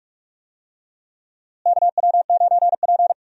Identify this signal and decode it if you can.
Morse code, heard as a keyed tone, about 42 wpm, 700 Hz. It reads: KW9P
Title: USA Call Signs - 4 Characters - Repeated 3x - 45wpm